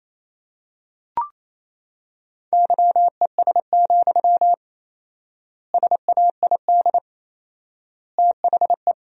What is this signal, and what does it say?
Morse code, heard as a keyed tone, about 28 wpm, 700 Hz. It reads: YES, SAID THE
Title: 4x Word-Spacing - Sentences from Top 500 Words - 28wpm